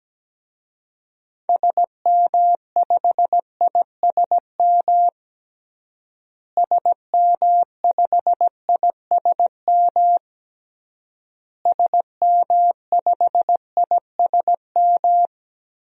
Morse code, heard as a keyed tone, about 17 wpm, 700 Hz. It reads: SM5ISM SM5ISM SM5ISM
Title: DXCC-100 Call Signs - 6 Characters - Repeated 3x - 17wpm